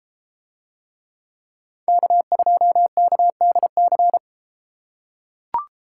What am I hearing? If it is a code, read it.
Morse code, heard as a keyed tone, about 31 wpm, 700 Hz. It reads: K2KDC